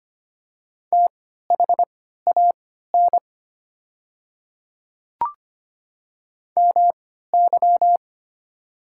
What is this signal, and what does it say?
Morse code, heard as a keyed tone, about 25 wpm, spaced as for 14 wpm, 700 Hz. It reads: THAN MY
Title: 3x Character Spacing - Top 100 Words - 25wpm